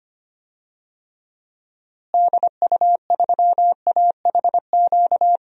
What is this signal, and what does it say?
Morse code, heard as a keyed tone, about 25 wpm, 700 Hz. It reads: DU3AHQ
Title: DXCC-100 Call Signs - 6 Characters - Repeated 2x - 25wpm